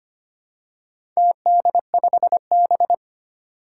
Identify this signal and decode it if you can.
Morse code, heard as a keyed tone, about 25 wpm, 700 Hz. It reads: TD5B
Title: Four Challenging Letter-Number Combinations 25wpm